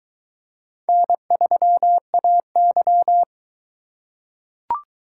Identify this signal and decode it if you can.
Morse code, heard as a keyed tone, about 22 wpm, 700 Hz. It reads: N3AY